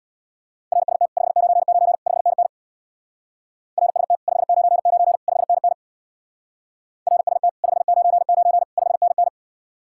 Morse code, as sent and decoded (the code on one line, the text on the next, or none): .-. ... - / ..... ----. ----. / ..... -. -. / .-. ... - / ..... ----. ----. / ..... -. -. / .-. ... - / ..... ----. ----. / ..... -. -.
RST 599 5NN RST 599 5NN RST 599 5NN